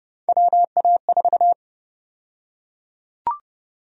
Morse code, heard as a keyed tone, about 30 wpm, 700 Hz. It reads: WA4
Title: USA Call Sign Prefixes - 30wpm